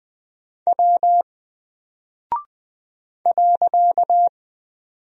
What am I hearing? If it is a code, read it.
Morse code, heard as a keyed tone, about 20 wpm, 700 Hz. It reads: W .
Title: Single Character (Letter:Number:Punctuation:Prosign) Alt 20wpm